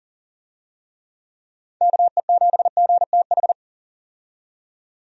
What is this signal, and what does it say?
Morse code, heard as a keyed tone, about 40 wpm, 700 Hz. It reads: KE7GTH
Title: USA Call Signs - 6 Characters - 40wpm